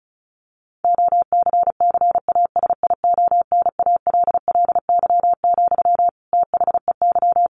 Morse code, heard as a keyed tone, about 34 wpm, 700 Hz. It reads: OCCASIONALLY, THEY
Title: Sentences from Top 1400 Words 35wpm